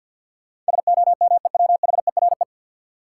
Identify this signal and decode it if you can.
Morse code, heard as a keyed tone, about 48 wpm, 700 Hz. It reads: SOMEWHERE
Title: Sentences from Top 1400 Words - New Word Drill - 50wpm